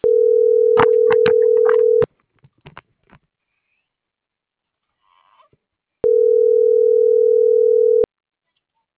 Ringback tone (North America), 440 + 480 Hz.
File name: CENTRAL SAGOT / out miriam.mercado